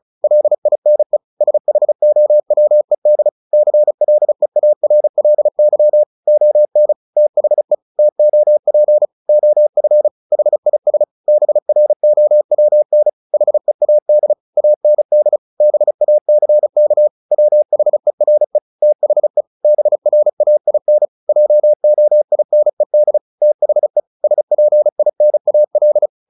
Morse code, heard as a keyed tone, about 34 wpm, 600 Hz. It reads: LINE SHOWED CLEARLY ON THE TOP OF HIS BROWN HEAD AND BACK WHERE THE BRAIN JOINED THE SPINAL
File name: the old man and the sea 2 35wpm